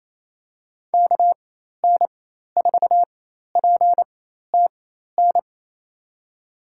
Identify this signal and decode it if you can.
Morse code, heard as a keyed tone, about 28 wpm, spaced as for 13 wpm, 700 Hz. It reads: KN4PTN
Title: USA Call Signs - 6 Characters - 4x Character Spacing - 28wpm